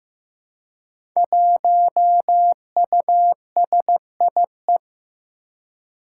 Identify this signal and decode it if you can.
Morse code, heard as a keyed tone, about 15 wpm, 700 Hz. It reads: 1USIE
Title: Five Challenging Letter-Number Combinations 15wpm